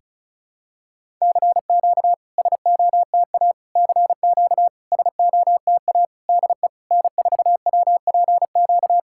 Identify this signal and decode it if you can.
Morse code, heard as a keyed tone, about 36 wpm, 690 Hz. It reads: CQ SOTA CQ SOTA DE N4WPQ
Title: SOTA QSO Practice 35wpm